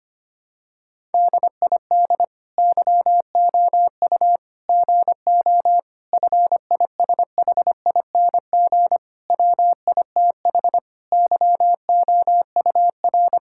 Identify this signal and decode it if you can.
Morse code, heard as a keyed tone, about 25 wpm, 700 Hz. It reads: DID YOU GO FISHING WITH YOUR
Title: Sentences from Top 1000 Words - Review Part 5 - 25wpm